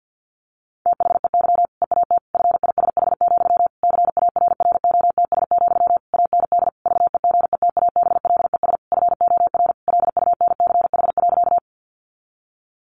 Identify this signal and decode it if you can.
Morse code, heard as a keyed tone, about 50 wpm, 700 Hz. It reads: THEY EAT FISH, CARROTS, AND VEGETABLES FOR LUNCH.
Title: Easy Phrases 50wpm